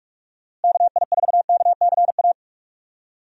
Morse code, heard as a keyed tone, about 47 wpm, 700 Hz. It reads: KI4KCA
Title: USA Call Signs - 6 Characters - Repeated 3x - 45wpm